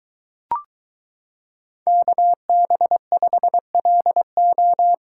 Morse code, heard as a keyed tone, about 22 wpm, 700 Hz. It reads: KB5LO